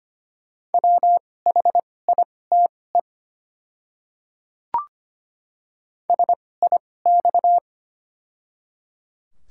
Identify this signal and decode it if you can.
Morse code, heard as a keyed tone, about 25 wpm, spaced as for 18 wpm, 700 Hz. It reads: WHITE SIX
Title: 2x Character Spacing - Top 500 Words - 25wpm